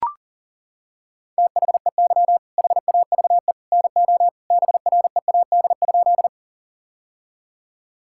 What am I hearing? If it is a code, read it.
Morse code, heard as a keyed tone, about 40 wpm, 700 Hz. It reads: THEY HAVE NO BREAD?
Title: Sentences from Top 1600 Words 40wpm